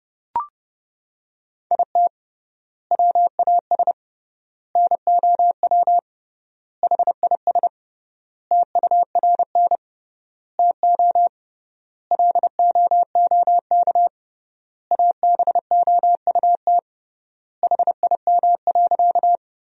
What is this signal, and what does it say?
Morse code, heard as a keyed tone, about 30 wpm, 700 Hz. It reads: IT WAS NOW HIS TURN TO LOOK ABOUT HIM.
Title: 3x Word-Spacing - Sentences from Top 200 Words - 30wpm